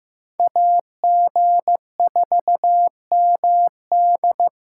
Morse code, heard as a keyed tone, about 18 wpm, 700 Hz. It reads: AG4MD
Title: Speed Racing - USA 2x2 Call Signs 15wpm